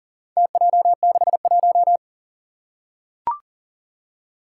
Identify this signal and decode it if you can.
Morse code, heard as a keyed tone, about 39 wpm, 700 Hz. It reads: TJ61
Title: Four Challenging Letter-Number Combinations 40wpm